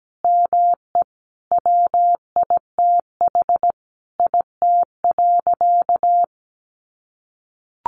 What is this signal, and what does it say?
Morse code, heard as a keyed tone, about 17 wpm, 700 Hz. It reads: ME WITH IT.
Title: Sentences from Top 1400 Words 17wpm